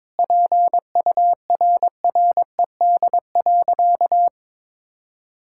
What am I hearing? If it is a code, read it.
Morse code, heard as a keyed tone, about 22 wpm, 700 Hz. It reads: PURRED.
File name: practice / 22wpm / Sentences with 2-Word Phrases - 22wpm